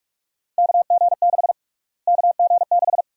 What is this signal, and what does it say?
Morse code, heard as a keyed tone, about 46 wpm, 700 Hz. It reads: KG6 KG6
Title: USA Call Sign Prefixes - Repeated 2x - 45wpm